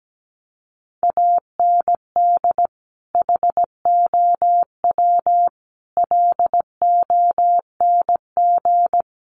Morse code, heard as a keyed tone, about 17 wpm, 700 Hz. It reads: AND HOW LONG
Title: Sentences from Top 1500 Words 17wpm